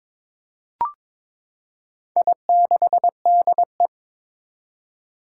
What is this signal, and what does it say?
Morse code, heard as a keyed tone, about 22 wpm, 700 Hz. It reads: I6DE
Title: Four Challenging Letter-Number Combinations 22wpm